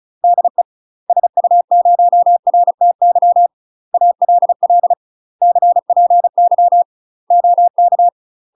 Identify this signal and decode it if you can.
Morse code, heard as a keyed tone, about 34 wpm, 700 Hz. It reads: DE SU0RTY ALL CPY OK
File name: UK MASTER QSOs TRAINING / 35WPM / UK MASTER QSOs 0040